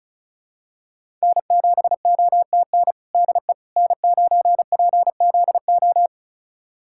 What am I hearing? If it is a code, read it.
Morse code, heard as a keyed tone, about 36 wpm, 700 Hz. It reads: N7OTN DE N9PZO
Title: USA Callsign DE Callsign - 5 Characters - 35wpm